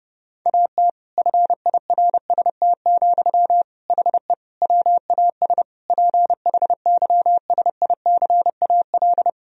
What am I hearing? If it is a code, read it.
Morse code, heard as a keyed tone, about 30 wpm, 700 Hz. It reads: AT FIRST, HE WAS PHYSICAL
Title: Sentences from Top 1000 Words - Review Part 4 - 30wpm